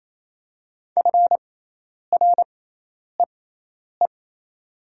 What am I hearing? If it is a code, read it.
Morse code, heard as a keyed tone, about 28 wpm, spaced as for 10 wpm, 700 Hz. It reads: FREE
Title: 6x Character Spacing - Top 500 Words - 28wpm